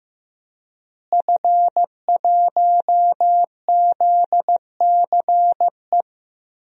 Morse code, heard as a keyed tone, about 15 wpm, 700 Hz. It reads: F1ZCE
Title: DXCC-100 Call Signs - 5 Characters - Repeated 2x - 15wpm